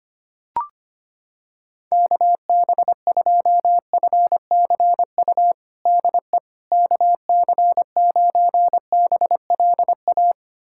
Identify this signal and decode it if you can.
Morse code, heard as a keyed tone, about 25 wpm, 700 Hz. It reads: KB2FCU DE KC9BLA
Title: USA Callsign DE Callsign 25wpm